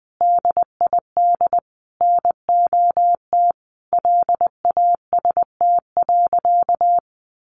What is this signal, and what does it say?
Morse code, heard as a keyed tone, about 20 wpm, 700 Hz. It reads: DID NOT LAST.
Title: Sentences from Top 1200 Words 20wpm